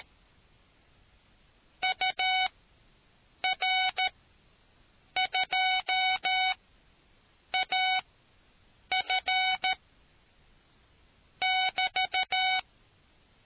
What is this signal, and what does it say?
Morse code, heard as a keyed tone, about 13 wpm, spaced as for 7 wpm, 750 Hz. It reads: UR2AF <BT>